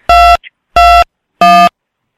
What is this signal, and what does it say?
DTMF, heard as telephone touch tones, tones of about 290 ms. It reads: 221